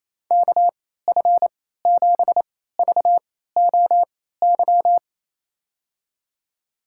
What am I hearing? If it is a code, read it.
Morse code, heard as a keyed tone, about 28 wpm, spaced as for 16 wpm, 700 Hz. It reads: KF7VOY
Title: USA Call Signs - 6 Characters - 3x Character Spacing - 28wpm